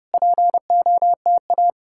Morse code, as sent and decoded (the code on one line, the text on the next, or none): .--. --- - .-
POTA